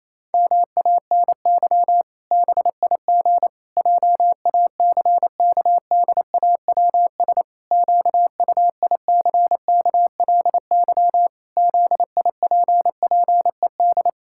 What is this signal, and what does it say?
Morse code, heard as a keyed tone, about 28 wpm, 700 Hz. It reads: MANY BIG JACKDAWS QUICKLY ZIPPED
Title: Pangram Challenge v2 - 28wpm